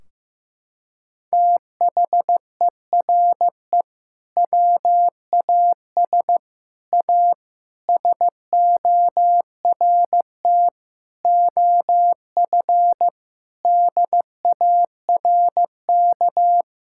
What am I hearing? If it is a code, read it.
Morse code, heard as a keyed tone, about 15 wpm, 700 Hz. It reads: THERE WAS A SORT OF DARK
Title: Sentences from Top 1600 Words 15wpm